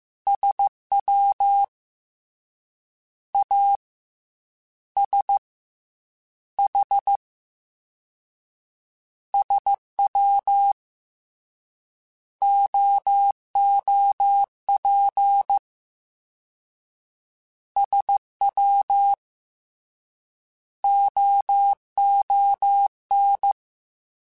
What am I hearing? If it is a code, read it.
Morse code, heard as a keyed tone, about 15 wpm, 800 Hz. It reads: SW A S H SW OOP SW OON
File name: Misc / Misc-mp3s / Jumps @13